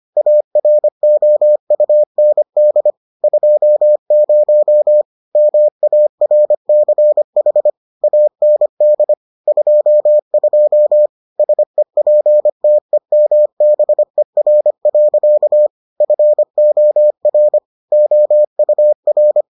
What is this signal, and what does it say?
Morse code, heard as a keyed tone, about 25 wpm, 600 Hz. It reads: AROUND 20 MARCH AND 22 SEPTEMBER. FOR OUR